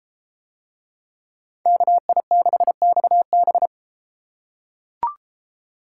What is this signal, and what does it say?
Morse code, heard as a keyed tone, about 33 wpm, 700 Hz. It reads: KI6XB